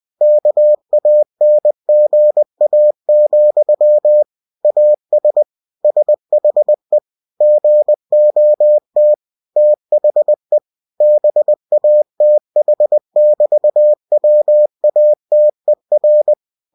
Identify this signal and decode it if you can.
Morse code, heard as a keyed tone, about 20 wpm, 600 Hz. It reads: KANGA, AS SHE GOT THE BATH<BT>WATER
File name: cw-books / winnie-the-pooh / winnie-20wpm-page-0071